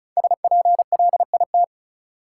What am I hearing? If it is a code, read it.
Morse code, heard as a keyed tone, about 36 wpm, 700 Hz. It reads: SPLIT
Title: Five Letter Words - 35wpm